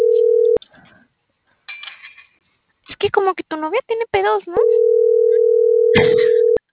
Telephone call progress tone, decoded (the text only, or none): ringback tone